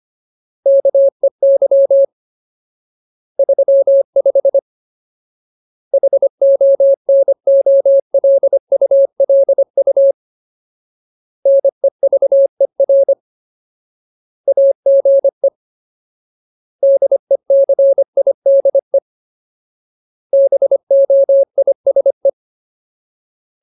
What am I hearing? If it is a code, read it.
Morse code, heard as a keyed tone, about 25 wpm, 550 Hz. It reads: KEY 35 HONOLULU NEVER AGE DECIDE BOISE
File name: Ultimate Word List / 25WPM Ultimate